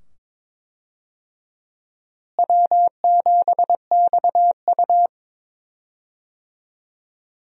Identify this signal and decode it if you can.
Morse code, heard as a keyed tone, about 22 wpm, 700 Hz. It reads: W7XU
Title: USA Call Signs - 4 Characters - Repeated 2x - 22wpm